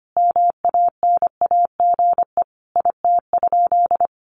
Morse code, heard as a keyed tone, about 25 wpm, 700 Hz. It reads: MANAGE IT?
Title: Sentences from Top 1400 Words 25wpm